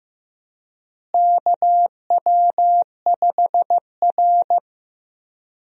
Morse code, heard as a keyed tone, about 15 wpm, 700 Hz. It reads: KW5R
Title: USA Call Signs - 4 Characters - Repeated 3x - 15wpm